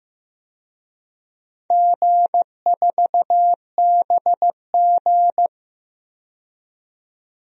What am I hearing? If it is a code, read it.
Morse code, heard as a keyed tone, about 15 wpm, 700 Hz. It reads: G4BG
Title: DXCC-100 Call Signs - 4 Characters - 15wpm